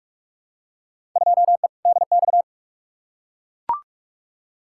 Morse code, heard as a keyed tone, about 44 wpm, 700 Hz. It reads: 2E DX
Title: Field Day Exchanges - 45wpm